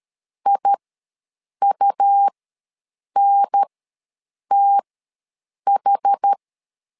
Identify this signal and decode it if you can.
Morse code, heard as a keyed tone, about 13 wpm, spaced as for 7 wpm, 800 Hz. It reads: IUNTH
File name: LEZIONI / 2 UHDN / 2 UHDN ws 12wpm 1708